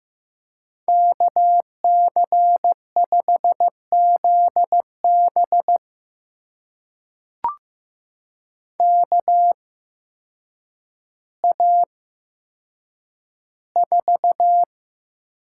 Morse code, heard as a keyed tone, about 15 wpm, spaced as for 8 wpm, 700 Hz. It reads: KC5ZB KA4
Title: USA Call Signs - 8x Character Spacing - 15wpm